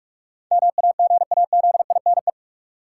Morse code, heard as a keyed tone, about 42 wpm, 700 Hz. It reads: MAGAZINE
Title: Sentences from Top 1000 Words - Review Part 7 - New Word Drill - 45wpm